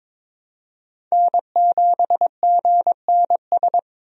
Morse code, heard as a keyed tone, about 22 wpm, 700 Hz. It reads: N7GNS
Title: USA Call Signs - 5 Characters - 22wpm